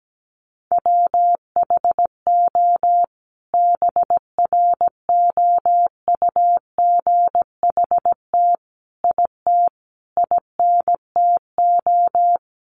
Morse code, heard as a keyed tone, about 17 wpm, 700 Hz. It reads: WHO BROUGHT IT INTO
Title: Sentences from Top 600 Words 17wpm